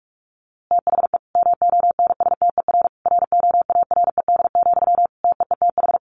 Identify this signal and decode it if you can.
Morse code, heard as a keyed tone, about 44 wpm, 700 Hz. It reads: THE MONSTER ROARED, TEETH
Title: A Sound Of Thunder 45wpm